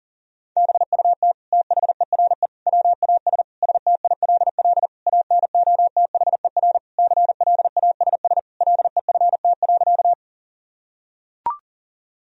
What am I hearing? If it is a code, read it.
Morse code, heard as a keyed tone, about 40 wpm, 700 Hz. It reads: BUT THERE WAS STILL ANOTHER CLASS LEFT.
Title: Sentences from Top 1000 Words - Review Part 2 - 40wpm